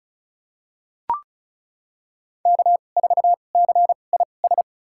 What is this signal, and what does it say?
Morse code, heard as a keyed tone, about 36 wpm, spaced as for 25 wpm, 700 Hz. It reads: K4CIS
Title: USA Call Signs - 5 Characters - 2x Character Spacing - 35wpm